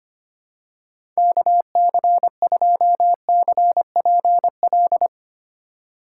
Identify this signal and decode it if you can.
Morse code, heard as a keyed tone, about 25 wpm, 700 Hz. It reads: KC2CPL
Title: USA Call Signs - 4x Character Spacing - 25wpm